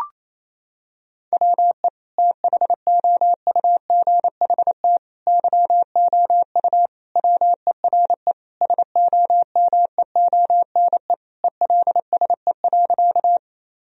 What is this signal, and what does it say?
Morse code, heard as a keyed tone, about 28 wpm, 700 Hz. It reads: WE THOUGHT YOU WERE SOMEONE ELSE.
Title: Sentences from Top 1000 Words - Review Part 4 - 28wpm